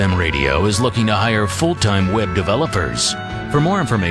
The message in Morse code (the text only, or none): none